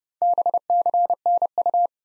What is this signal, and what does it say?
Morse code, heard as a keyed tone, about 30 wpm, 700 Hz. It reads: BCNU
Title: Top 90 Abbr 30WPM